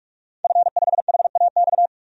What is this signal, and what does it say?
Morse code, heard as a keyed tone, about 41 wpm, 700 Hz. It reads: F5HAX
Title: DXCC-100 Call Signs - 5 Characters - 45wpm